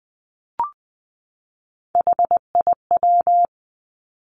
Morse code, heard as a keyed tone, about 20 wpm, 700 Hz. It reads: HIW
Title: Three Challenging Letter-Number Combinations 20wpm